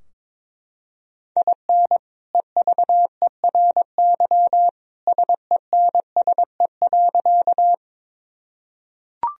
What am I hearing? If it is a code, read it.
Morse code, heard as a keyed tone, about 22 wpm, 700 Hz. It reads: IN EVERY SENSE.
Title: Sentences from Top 1000 Words - Review Part 4 - 22wpm